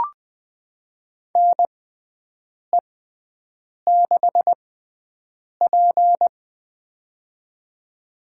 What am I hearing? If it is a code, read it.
Morse code, heard as a keyed tone, about 20 wpm, spaced as for 7 wpm, 700 Hz. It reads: NE6P
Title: USA Call Signs - 4 Characters - 6x Character Spacing - 20wpm